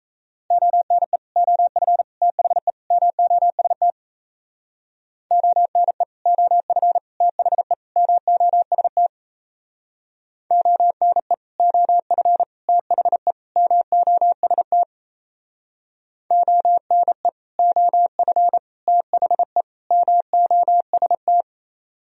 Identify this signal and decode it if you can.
Morse code, heard as a keyed tone, about 37 wpm, 700 Hz. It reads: ONE OF THE MOST ONE OF THE MOST ONE OF THE MOST ONE OF THE MOST